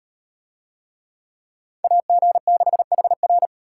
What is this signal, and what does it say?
Morse code, heard as a keyed tone, about 38 wpm, 700 Hz. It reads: AG6HR